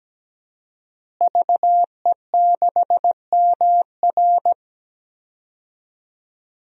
Morse code, heard as a keyed tone, about 17 wpm, 700 Hz. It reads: VE6MR
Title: DXCC-100 Call Signs - 5 Characters - Repeated 3x - 17wpm